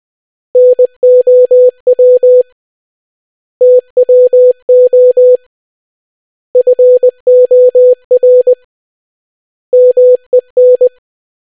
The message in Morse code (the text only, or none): NOW TWO FOR MEN